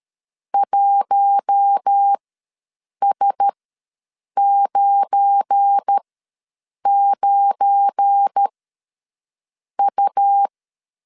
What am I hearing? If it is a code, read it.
Morse code, heard as a keyed tone, about 13 wpm, spaced as for 7 wpm, 800 Hz. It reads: 1S99U
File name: LEZIONI / 7 GF9Z / 7 GF9 ws 12wpm 3344